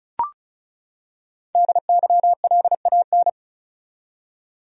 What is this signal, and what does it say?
Morse code, heard as a keyed tone, about 36 wpm, 700 Hz. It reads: DYLAN
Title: USA Common Names - 35wpm